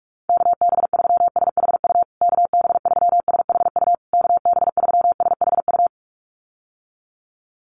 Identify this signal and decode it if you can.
Morse code, heard as a keyed tone, about 44 wpm, 700 Hz. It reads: KB3SHU KB3SHU KB3SHU
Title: USA Call Signs Repeated 3x 45wpm